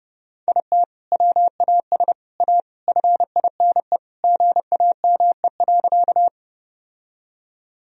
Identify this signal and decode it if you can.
Morse code, heard as a keyed tone, about 32 wpm, 700 Hz. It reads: IT WAS A FINE GAME.
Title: Sentences from Top 1000 Words - Review Part 3 - 30wpm